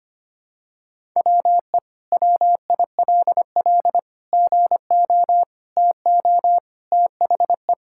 Morse code, heard as a keyed tone, about 25 wpm, 700 Hz. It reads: WE WILL GO TO THE
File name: practice / 25wpm / Sentences with 2-Word Phrases - 25wpm